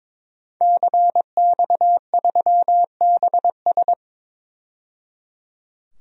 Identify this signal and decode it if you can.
Morse code, heard as a keyed tone, about 22 wpm, 700 Hz. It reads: CX3BS
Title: DXCC-100 Call Signs - 5 Characters - 22wpm